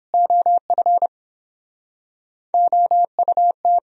Morse code, heard as a keyed tone, about 32 wpm, 700 Hz. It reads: OF OUT